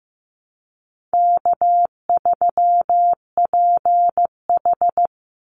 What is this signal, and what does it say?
Morse code, heard as a keyed tone, about 15 wpm, 700 Hz. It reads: K3PH
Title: Easy Call Signs Repeated 2x - 15wpm